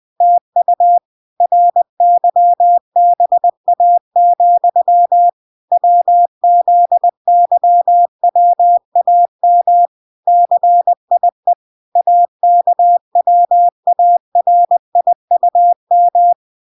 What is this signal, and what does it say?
Morse code, heard as a keyed tone, about 20 wpm, 700 Hz. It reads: TU RYBA, WZYWAM CIE AKWARIUM